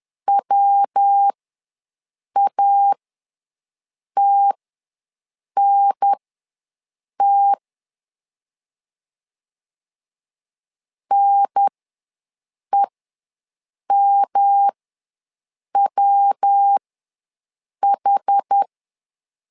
Morse code, heard as a keyed tone, about 10 wpm, spaced as for 6 wpm, 800 Hz. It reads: WATNT NEMWH
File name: LEZIONI / 3 MWRO / 3 MW ws 10wpm 2213